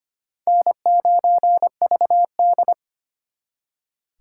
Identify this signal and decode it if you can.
Morse code, heard as a keyed tone, about 32 wpm, 700 Hz. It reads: N9VD